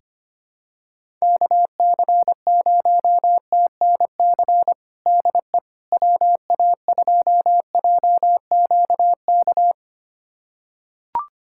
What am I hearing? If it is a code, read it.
Morse code, heard as a keyed tone, about 25 wpm, 700 Hz. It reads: KC0TNC DE WA2JQK
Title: USA Callsign DE Callsign - 6 Characters - 25wpm